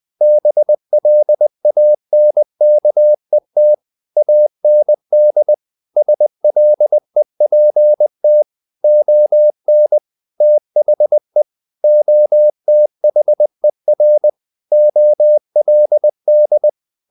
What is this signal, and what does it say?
Morse code, heard as a keyed tone, about 20 wpm, 600 Hz. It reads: BLANKET AND SLEPT ON THE OTHER OLD
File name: the old man and the sea / the old man and the sea part 1 20wpm